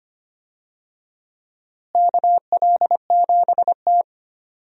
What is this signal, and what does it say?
Morse code, heard as a keyed tone, about 25 wpm, 700 Hz. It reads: KL7T
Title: DXCC-100 Call Signs - 4 Characters - Repeated 2x - 25wpm